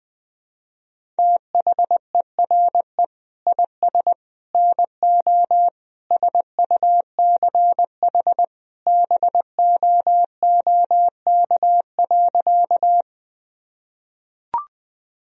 Morse code, heard as a keyed tone, about 20 wpm, 700 Hz. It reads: THERE IS NO SUCH BOOK.
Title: Sentences from Top 1000 Words - Review Part 1 - 20wpm